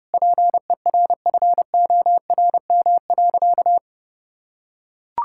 Morse code, heard as a keyed tone, about 30 wpm, 700 Hz. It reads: PERFORM.
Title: Sentences from Top 1000 Words - Review Part 5 - 30wpm